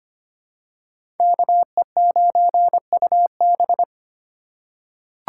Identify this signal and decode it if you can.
Morse code, heard as a keyed tone, about 25 wpm, 700 Hz. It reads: KE9UB
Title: USA Call Signs - 5 Characters - 3x Character Spacing - 25wpm